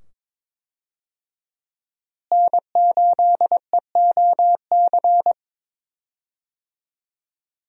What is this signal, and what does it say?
Morse code, heard as a keyed tone, about 22 wpm, 700 Hz. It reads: N8EOC
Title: USA Call Signs - 5 Characters - Repeated 2x - 22wpm